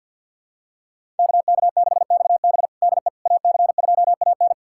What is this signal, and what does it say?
Morse code, heard as a keyed tone, about 51 wpm, 700 Hz. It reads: KK6XB DE AC2AN
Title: USA Callsign DE Callsign - 5 Characters - 50wpm